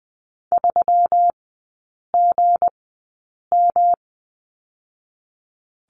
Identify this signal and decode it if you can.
Morse code, heard as a keyed tone, about 19 wpm, spaced as for 8 wpm, 700 Hz. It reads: 3GM
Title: 184 Lesson 30e - Letter b - Three Characters 20wpm